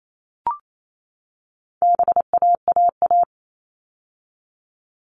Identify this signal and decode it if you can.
Morse code, heard as a keyed tone, about 28 wpm, 700 Hz. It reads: BAAA
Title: All Cut Numbers - 4-Digit 28wpm